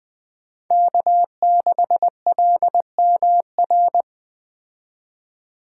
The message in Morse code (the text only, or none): K6LMR